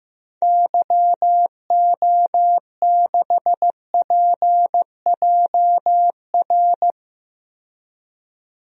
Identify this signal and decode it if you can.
Morse code, heard as a keyed tone, about 15 wpm, 700 Hz. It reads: YO6PJR
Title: DXCC-100 Call Signs - 6 Characters - 15wpm